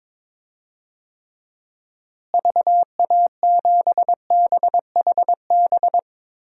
Morse code, heard as a keyed tone, about 22 wpm, 700 Hz. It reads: VA7BHB
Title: DXCC-100 Call Signs - 6 Characters - Repeated 3x - 22wpm